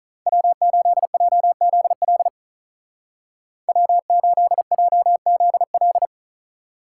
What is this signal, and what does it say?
Morse code, heard as a keyed tone, about 47 wpm, 700 Hz. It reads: W8JZL W8JZL